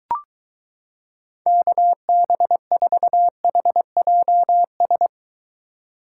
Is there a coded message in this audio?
Yes: Morse code from 1.5 to 5.1 s, heard as a keyed tone, about 23 wpm, 700 Hz. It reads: KB4HJS